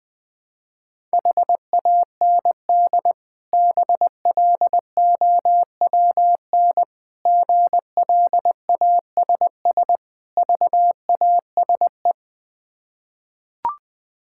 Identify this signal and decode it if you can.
Morse code, heard as a keyed tone, about 20 wpm, 690 Hz. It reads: HAND BLOWN GLASS VASE
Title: Difficult Words with Hints - 20wpm